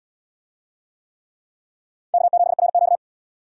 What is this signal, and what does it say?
Morse code, heard as a keyed tone, about 70 wpm, 700 Hz. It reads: K7RQ